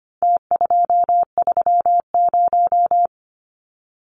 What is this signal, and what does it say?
Morse code, heard as a keyed tone, about 25 wpm, 700 Hz. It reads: T230
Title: Numbers - Contest Style - 4-Digit 25wpm